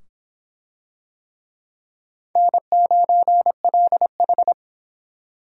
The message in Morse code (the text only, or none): N9LH